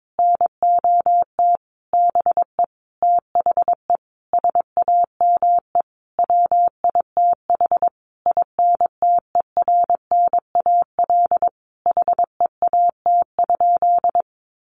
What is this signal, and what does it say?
Morse code, heard as a keyed tone, about 22 wpm, 700 Hz. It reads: NOT BE THE SAME WITH INTERNAL HEAT?
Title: Sentences from Top 1400 Words 22wpm